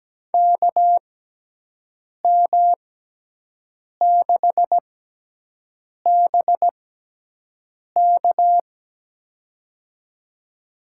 Morse code, heard as a keyed tone, about 17 wpm, spaced as for 6 wpm, 700 Hz. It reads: KM6BK
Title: USA Call Signs - 6x Character Spacing - 17wpm